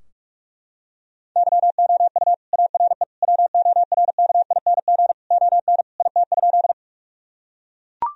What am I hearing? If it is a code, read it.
Morse code, heard as a keyed tone, about 43 wpm, 700 Hz. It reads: YOU ARE WORKING ON IT?
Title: Sentences from Top 1000 Words - Review Part 4 - 45wpm